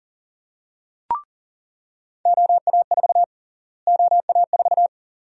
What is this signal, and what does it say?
Morse code, heard as a keyed tone, about 39 wpm, 700 Hz. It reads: OA4 OA4
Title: USA Call Sign Prefixes - Repeated 2x - 40wpm